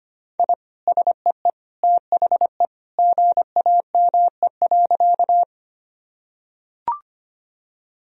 Morse code, heard as a keyed tone, about 25 wpm, 700 Hz. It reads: I SEE THE GAME.
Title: Sentences from Top 1000 Words - Review Part 3 - 25wpm